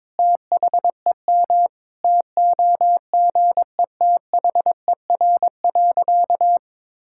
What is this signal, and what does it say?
Morse code, heard as a keyed tone, about 22 wpm, 700 Hz. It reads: THEM TOGETHER.
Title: Sentences from Top 1600 Words 22wpm